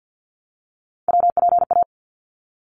Morse code, heard as a keyed tone, about 50 wpm, 700 Hz. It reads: WPA
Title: ARRL Sections 50wpm